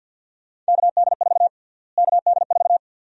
Morse code, heard as a keyed tone, about 51 wpm, 700 Hz. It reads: KD4 KD4
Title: USA Call Sign Prefixes - Repeated 2x - 50wpm